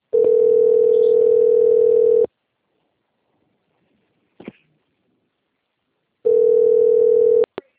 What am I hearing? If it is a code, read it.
Ringback tone (North America), 440 + 480 Hz.
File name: rg-1600-unknown- - - .17032